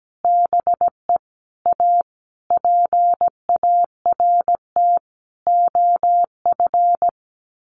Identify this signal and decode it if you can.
Morse code, heard as a keyed tone, about 17 wpm, 700 Hz. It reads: BE A PART OF
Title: Sets of 5 Words 17wpm